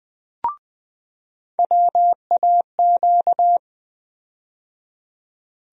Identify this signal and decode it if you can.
Morse code, heard as a keyed tone, about 20 wpm, 700 Hz. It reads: WAQ
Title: USA Call Sign Postfixes - 3 Letters 20wpm